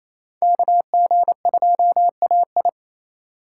Morse code, heard as a keyed tone, about 27 wpm, 700 Hz. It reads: KG2AI
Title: USA Call Signs - 5 Characters - 4x Character Spacing - 28wpm